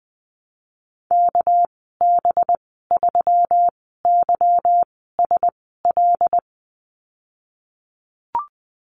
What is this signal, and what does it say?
Morse code, heard as a keyed tone, about 20 wpm, spaced as for 15 wpm, 700 Hz. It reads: KB3YSL
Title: 219 Lesson 35g - Number 0 - Call Signs 20wpm